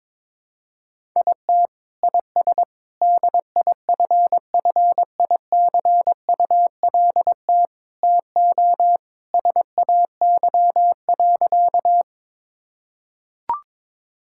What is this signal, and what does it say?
Morse code, heard as a keyed tone, about 22 wpm, 700 Hz. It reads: IT IS DIFFICULT TO SAY.
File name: practice / 22wpm / Sentences with 4-Word Phrases - 22wpm